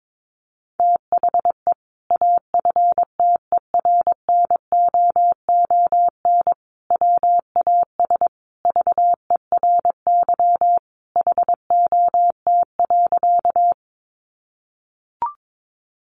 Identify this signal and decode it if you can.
Morse code, heard as a keyed tone, about 22 wpm, 700 Hz. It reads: THE AFTERNOON WAS VERY HOT.
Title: Sentences from Top 1500 Words 22wpm